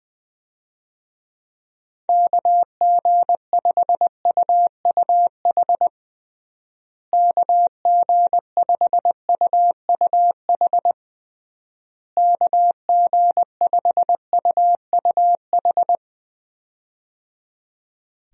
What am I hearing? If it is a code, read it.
Morse code, heard as a keyed tone, about 21 wpm, 700 Hz. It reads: KG5UUH KG5UUH KG5UUH
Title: USA Call Signs - 6 Characters - Repeated 3x - 20wpm